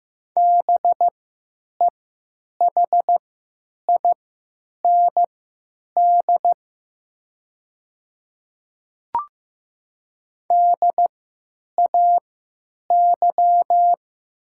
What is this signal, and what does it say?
Morse code, heard as a keyed tone, about 15 wpm, spaced as for 8 wpm, 700 Hz. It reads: BEHIND DAY
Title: 3x Character Spacing - Top 500 Words - 15wpm